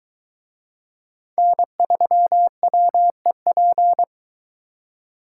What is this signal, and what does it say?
Morse code, heard as a keyed tone, about 22 wpm, 700 Hz. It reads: N3WEP